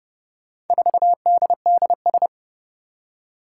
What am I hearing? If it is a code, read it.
Morse code, heard as a keyed tone, about 30 wpm, 700 Hz. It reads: 4DDS
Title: Four Challenging Letter-Number Combinations 30wpm